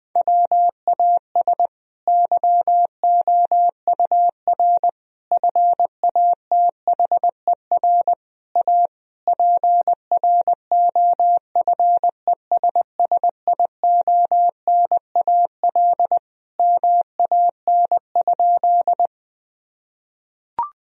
Morse code, heard as a keyed tone, about 20 wpm, 700 Hz. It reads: WAS YOUR FATHER A PROFESSIONAL MAN?
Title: Sentences from Top 1000 Words - Review Part 4 - 20wpm